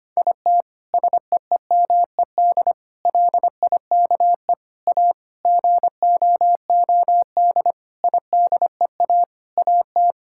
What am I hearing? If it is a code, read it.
Morse code, heard as a keyed tone, about 25 wpm, 700 Hz. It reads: IT SEEMED LIKE A GOOD IDEA AT
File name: Sets of 9 Words - 25wpm